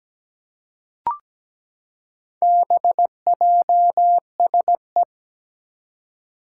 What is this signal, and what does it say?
Morse code, heard as a keyed tone, about 17 wpm, 700 Hz. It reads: BJSE
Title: Four Challenging Letter-Number Combinations 17wpm